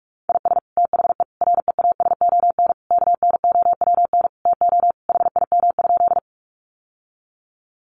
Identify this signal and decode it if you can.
Morse code, heard as a keyed tone, about 44 wpm, 700 Hz. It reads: IS THE REASON KNOWN TO HIM?
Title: Sentences from Top 700 Words 45wpm